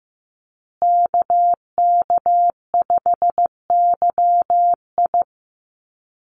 Morse code, heard as a keyed tone, about 15 wpm, 700 Hz. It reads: KK5YI
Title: USA Call Signs Repeated 2x 15wpm